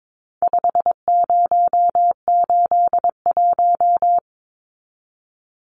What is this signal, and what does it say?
Morse code, heard as a keyed tone, about 22 wpm, 700 Hz. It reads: 5081
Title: Numbers - 4-Digit 22wpm